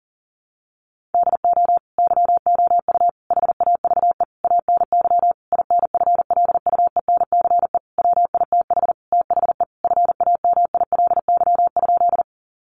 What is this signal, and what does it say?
Morse code, heard as a keyed tone, about 40 wpm, 700 Hz. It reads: DO YOU HAVE ANY INFLUENCE WITH THE FAMILY?
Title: Sentences from Top 1100 Words 40wpm